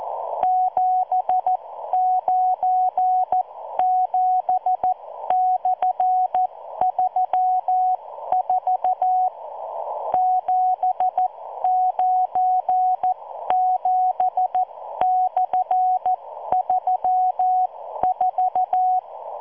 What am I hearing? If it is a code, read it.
Morse code, heard as a keyed tone, about 14 wpm, spaced as for 12 wpm, 740 Hz. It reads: 797/34 797/34